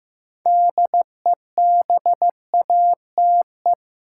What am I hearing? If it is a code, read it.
Morse code, heard as a keyed tone, about 15 wpm, 700 Hz. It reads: DEBATE
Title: Sentences from Top 1100 Words - New Word Drill - 15wpm